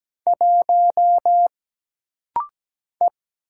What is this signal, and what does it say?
Morse code, heard as a keyed tone, about 17 wpm, 700 Hz. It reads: 1 E
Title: Single Letter-Number Alt - Rapid-Fire 17wpm